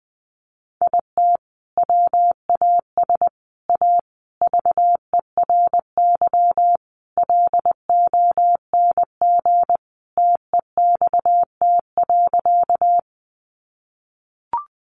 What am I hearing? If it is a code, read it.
Morse code, heard as a keyed tone, about 20 wpm, 700 Hz. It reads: IT WAS A VERY LONG TEXT.
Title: Sentences from Top 1200 Words 20wpm